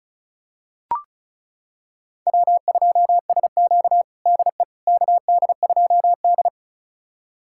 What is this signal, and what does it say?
Morse code, heard as a keyed tone, about 36 wpm, 700 Hz. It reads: W2SQ DE KD2D
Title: USA Callsign DE Callsign - 4 Characters - 35wpm